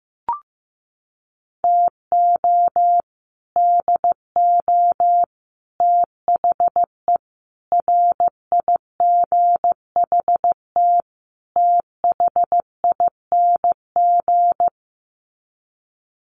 Morse code, heard as a keyed tone, about 14 wpm, 700 Hz. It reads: TO DO THE RIGHT THING
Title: Sets of 5 Words 15wpm